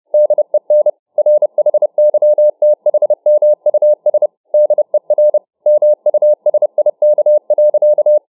Morse code, heard as a keyed tone, about 29 wpm, 600 Hz. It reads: DEN RHYTHMUS DER MUSIK.